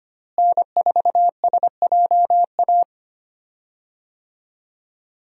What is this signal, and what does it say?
Morse code, heard as a keyed tone, about 25 wpm, 700 Hz. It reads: N4SJA
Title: USA Call Signs - 5 Characters - Repeated 2x - 25wpm